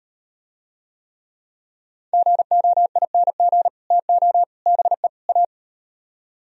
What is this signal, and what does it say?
Morse code, heard as a keyed tone, about 38 wpm, 700 Hz. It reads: GOING TO BE A